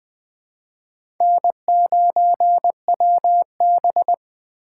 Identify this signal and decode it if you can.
Morse code, heard as a keyed tone, about 20 wpm, 700 Hz. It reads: N9WB
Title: USA Call Signs - 4 Characters - 8x Character Spacing - 20wpm